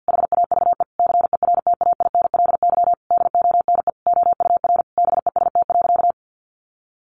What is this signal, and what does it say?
Morse code, heard as a keyed tone, about 50 wpm, 700 Hz. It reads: HAVE CERTAINLY DONE OUR BEST.
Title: Sentences from Top 900 Words 50wpm